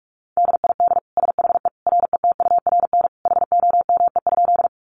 Morse code, heard as a keyed tone, about 44 wpm, 700 Hz. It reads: DID SHE RETURN HOME?
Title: Sentences from Top 800 Words 45wpm